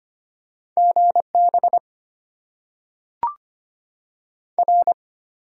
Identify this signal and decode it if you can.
Morse code, heard as a keyed tone, about 25 wpm, 690 Hz. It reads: GB R
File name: Top 90 QSO Elements - 8x Character Spacing - 25wpm